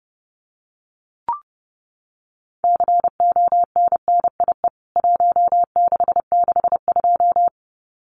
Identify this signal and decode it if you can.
Morse code, heard as a keyed tone, about 30 wpm, 700 Hz. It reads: CONNIE 1662
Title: CWT Exchange 30wpm